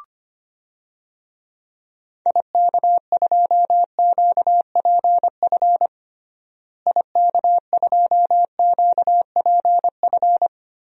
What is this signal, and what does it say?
Morse code, heard as a keyed tone, about 25 wpm, 700 Hz. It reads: IK2QPF IK2QPF
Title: DXCC-100 Call Signs - 6 Characters - Repeated 2x - 25wpm